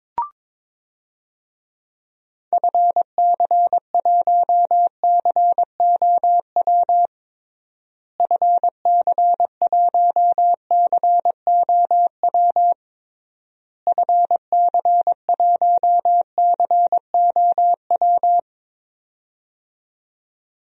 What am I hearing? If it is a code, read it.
Morse code, heard as a keyed tone, about 22 wpm, 700 Hz. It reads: FC1COW FC1COW FC1COW
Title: DXCC-100 Call Signs - 6 Characters - Repeated 3x - 22wpm